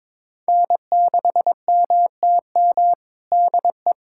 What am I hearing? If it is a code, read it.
Morse code, heard as a keyed tone, about 22 wpm, 700 Hz. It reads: N6MTM DE
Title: USA Callsign DE Callsign - 5 Characters - 22wpm